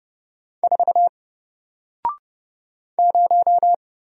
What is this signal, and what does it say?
Morse code, heard as a keyed tone, about 29 wpm, 700 Hz. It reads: 4 0
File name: Single-Letter-Number-30wpm